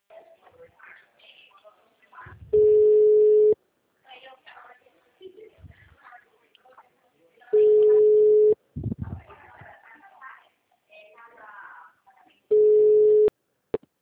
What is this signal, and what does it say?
Ringback tone (Europe, 425 Hz), with speech.